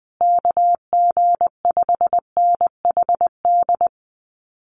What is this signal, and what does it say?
Morse code, heard as a keyed tone, about 20 wpm, 700 Hz. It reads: KG5NHD
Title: Common QSO Phrases Encore 20wpm